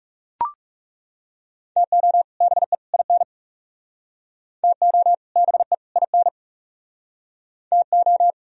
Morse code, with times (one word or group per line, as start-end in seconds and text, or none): 1.8-2.2 TO
2.4-2.7 BE
2.9-3.2 IN
4.6-5.1 TO
5.4-5.7 BE
6.0-6.3 IN
7.7-8.3 TO